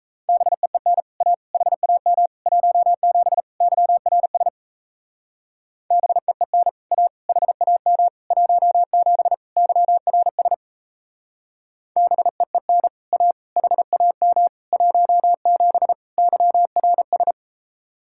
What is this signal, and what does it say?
Morse code, heard as a keyed tone, about 34 wpm, 700 Hz. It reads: BEEN A HAM 17 YRS BEEN A HAM 17 YRS BEEN A HAM 17 YRS
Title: Speed-Racing Common QSO Phrases Encore - 28wpm